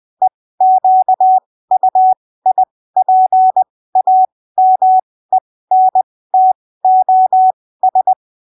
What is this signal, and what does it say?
Morse code, heard as a keyed tone, about 20 wpm, spaced as for 15 wpm, 750 Hz. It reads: EQUIPAMENTOS
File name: arquivos / texto / texto 13 20ppm